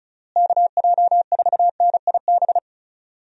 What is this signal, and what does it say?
Morse code, heard as a keyed tone, about 34 wpm, 700 Hz. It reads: KJ4NIB
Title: USA Call Signs - 6 Characters - Repeated 3x - 35wpm